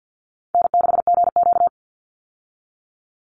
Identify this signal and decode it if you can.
Morse code, heard as a keyed tone, about 50 wpm, 700 Hz. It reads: N6GQ
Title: Easy Call Signs Repeated 3x - 50wpm